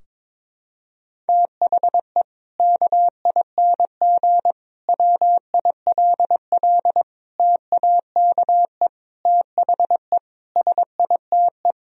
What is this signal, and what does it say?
Morse code, heard as a keyed tone, about 22 wpm, 700 Hz. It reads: THE KING WILL TAKE THE SITE
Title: Sentences from Top 1000 Words - Review Part 4 - 22wpm